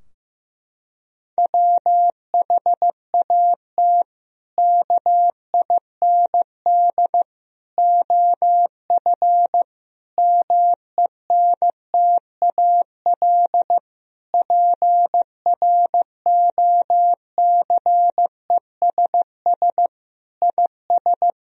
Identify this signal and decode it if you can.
Morse code, heard as a keyed tone, about 15 wpm, 700 Hz. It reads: WHAT KIND OF MENTAL PROCESS IS
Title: Sentences from Top 1000 Words - Review Part 4 - 15wpm